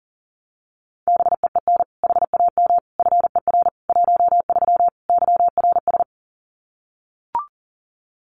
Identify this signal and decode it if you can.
Morse code, heard as a keyed tone, about 40 wpm, 700 Hz. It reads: BEEN HAM FER 13 YRS
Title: Common QSO Phrases 40wpm